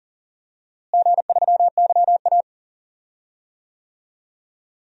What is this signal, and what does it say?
Morse code, heard as a keyed tone, about 37 wpm, 700 Hz. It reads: G3YA
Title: DXCC-100 Call Signs - 4 Characters - Repeated 3x - 40wpm